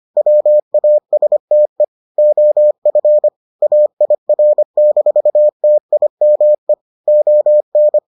Morse code, heard as a keyed tone, about 25 wpm, 600 Hz. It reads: WASTE OF AIR-TIME ON